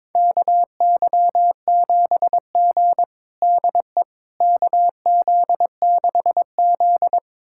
Morse code, heard as a keyed tone, about 21 wpm, 700 Hz. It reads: KY7G DE KZ6Z
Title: USA Callsign DE Callsign - 4 Characters - 22wpm